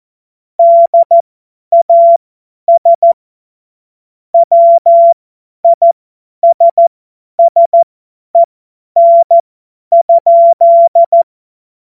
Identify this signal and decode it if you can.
Morse code, heard as a keyed tone, about 14 wpm, spaced as for 10 wpm, 680 Hz. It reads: DAS WISSEN?
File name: Witz-29-Juli-2025-50-F70